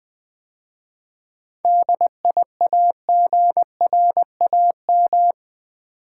Morse code, heard as a keyed tone, about 20 wpm, 700 Hz. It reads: DIAGRAM
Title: 7 Letter Words - First 3 Letters Given - 20wpm